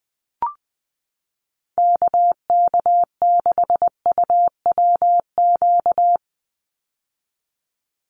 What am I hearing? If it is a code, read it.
Morse code, heard as a keyed tone, about 20 wpm, 700 Hz. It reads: KK6UWQ
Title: USA Call Signs 20wpm